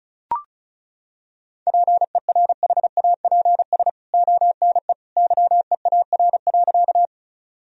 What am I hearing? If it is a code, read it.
Morse code, heard as a keyed tone, about 34 wpm, 700 Hz. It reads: PERHAPS ONE YEAR.
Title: Sentences from Top 1000 Words - Review Part 1 - 35wpm